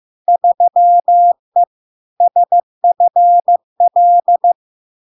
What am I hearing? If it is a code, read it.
Morse code, heard as a keyed tone, about 15 wpm, 700 Hz. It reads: 3E SFL
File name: LARK-FD-2022-Log-in-CW-15wpm-700-Hz